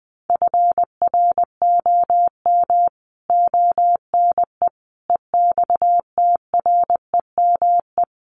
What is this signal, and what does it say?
Morse code, heard as a keyed tone, about 19 wpm, 700 Hz. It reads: FROM ONE EXTREME
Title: Sentences from Top 1300 Words 20wpm